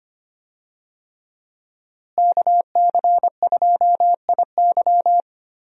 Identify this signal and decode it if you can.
Morse code, heard as a keyed tone, about 25 wpm, 700 Hz. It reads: KC2IY
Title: USA Call Signs - 5 Characters - Repeated 3x - 25wpm